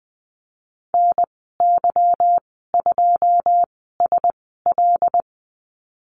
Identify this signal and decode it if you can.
Morse code, heard as a keyed tone, about 20 wpm, spaced as for 15 wpm, 700 Hz. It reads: NY2SL
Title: 173 Lesson 28g - Number 9 - Call Signs 20wpm